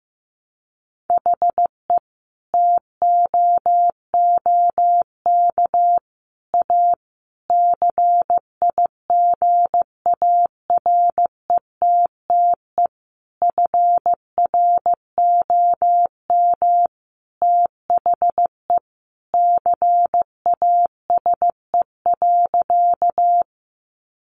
Morse code, heard as a keyed tone, about 15 wpm, 700 Hz. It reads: HE TOOK A CIGARETTE FROM THE CASE.
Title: Sentences from Top 1000 Words 15wpm